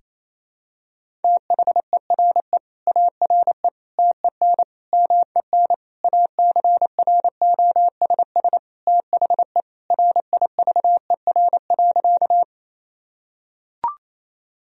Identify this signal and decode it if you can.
Morse code, heard as a keyed tone, about 28 wpm, 700 Hz. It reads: THERE ARE TEN MEN ACROSS THE RIVER.
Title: Sentences from Top 1000 Words - Review Part 2 - 28wpm